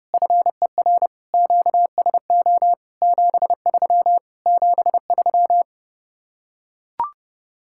Morse code, heard as a keyed tone, about 30 wpm, 700 Hz. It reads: FER QSO 73 73